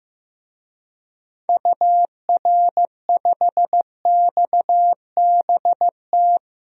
Morse code, heard as a keyed tone, about 15 wpm, 700 Hz. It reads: UR5XBT
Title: DXCC-100 Call Signs - 6 Characters - Repeated 3x - 15wpm